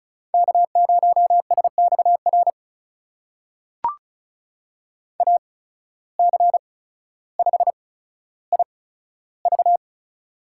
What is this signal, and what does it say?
Morse code, heard as a keyed tone, about 36 wpm, spaced as for 15 wpm, 700 Hz. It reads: K0SXR AC5IV
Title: USA Call Signs - 5 Characters - 8x Character Spacing - 35wpm